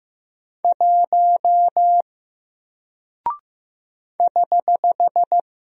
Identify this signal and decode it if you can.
Morse code, heard as a keyed tone, about 15 wpm, 700 Hz. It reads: 1 <HH>
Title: Single Character (Letter:Number:Punctuation:Prosign) Alt 15wpm